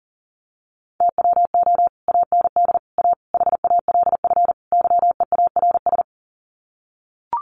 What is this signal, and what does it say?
Morse code, heard as a keyed tone, about 40 wpm, 700 Hz. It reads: TWO AND A HALF YEARS
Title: Sets of 5 Words 40wpm